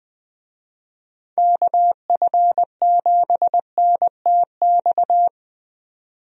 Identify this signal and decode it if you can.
Morse code, heard as a keyed tone, about 21 wpm, 700 Hz. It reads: KF7NTX
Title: USA Call Signs - 6 Characters - 20wpm